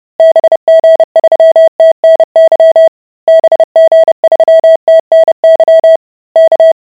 Morse code, heard as a keyed tone, about 30 wpm, 650 Hz. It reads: BG3TNY BG3TNY K